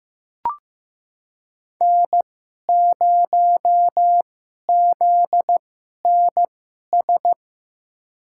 Morse code, heard as a keyed tone, about 15 wpm, spaced as for 11 wpm, 700 Hz. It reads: N0ZNS
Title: USA Call Signs - 5 Characters - 2x Character Spacing - 15wpm